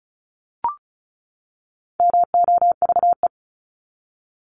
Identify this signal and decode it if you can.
Morse code, heard as a keyed tone, about 34 wpm, 700 Hz. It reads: MOVE
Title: Top 200 Words 35wpm